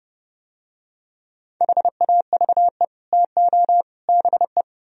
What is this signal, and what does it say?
Morse code, heard as a keyed tone, about 30 wpm, 700 Hz. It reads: HAVE TO BE